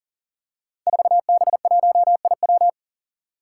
Morse code, heard as a keyed tone, about 40 wpm, 700 Hz. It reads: 4B1IW
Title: Five Challenging Letter-Number Combinations 40wpm